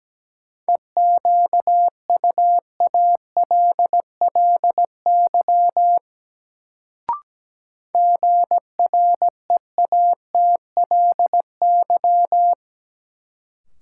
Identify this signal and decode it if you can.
Morse code, heard as a keyed tone, about 17 wpm, 700 Hz. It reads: EQUALLY GREATLY
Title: Sentences from Top 1400 Words - New Word Drill - 17wpm